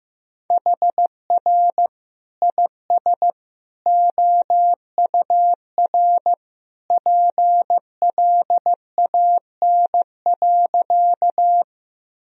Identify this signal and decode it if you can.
Morse code, heard as a keyed tone, about 15 wpm, 700 Hz. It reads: HR IS OUR PLAN.
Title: Sentences from Top 500 Words and Abbreviations 15wpm